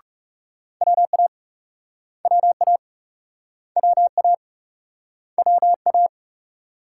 Morse code, heard as a keyed tone, about 42 wpm, 700 Hz. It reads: WA WA WA WA